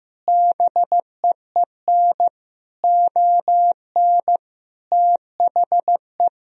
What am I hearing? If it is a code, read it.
Morse code, heard as a keyed tone, about 16 wpm, 700 Hz. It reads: BEEN ON THE
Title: Sentences from Top 1000 Words - Review Part 1 - 15wpm